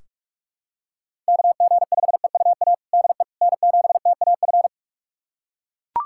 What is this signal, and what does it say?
Morse code, heard as a keyed tone, about 41 wpm, 700 Hz. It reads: KG5EUA DE N7TAF
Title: USA Callsign DE Callsign 45wpm